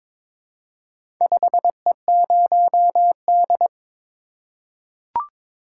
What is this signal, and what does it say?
Morse code, heard as a keyed tone, about 22 wpm, 700 Hz. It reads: 5E0D
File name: practice / 22wpm / Four Letter-Number Combinations - 22wpm